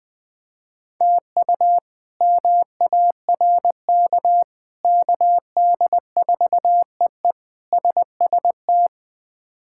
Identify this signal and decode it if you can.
Morse code, heard as a keyed tone, about 21 wpm, 700 Hz. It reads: TU MARK KD4EE SST
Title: K1USN SST QSO Practice - 20wpm